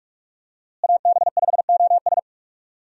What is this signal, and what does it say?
Morse code, heard as a keyed tone, about 41 wpm, 700 Hz. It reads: AB5OS
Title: USA Call Signs - 5 Characters - Repeated 2x - 45wpm